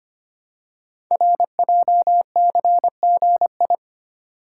Morse code, heard as a keyed tone, about 25 wpm, 700 Hz. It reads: RJCGI
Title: Code Groups - 5 Letters - 25wpm